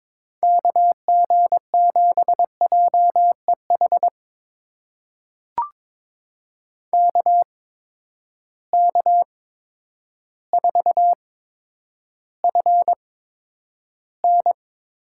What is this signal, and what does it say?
Morse code, heard as a keyed tone, about 22 wpm, spaced as for 10 wpm, 700 Hz. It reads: KG7JEH KK4FN
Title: USA Call Signs - 6 Characters - 8x Character Spacing - 22wpm